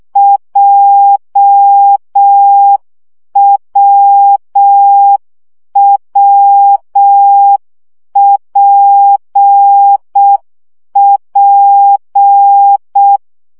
Morse code, heard as a keyed tone, about 6 wpm, 800 Hz. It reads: JWWPP
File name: CW gyakorlas w p j 5 baud